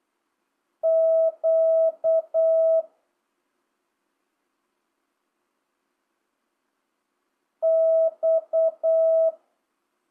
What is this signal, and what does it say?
Morse code, heard as a keyed tone, about 8 wpm, 650 Hz. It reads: Q X
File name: jh8cbh / 2025cw